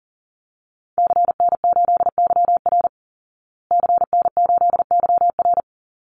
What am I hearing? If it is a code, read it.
Morse code, heard as a keyed tone, about 40 wpm, 700 Hz. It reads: CN8YR CN8YR
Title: Call Signs Repeated 2x - 40wpm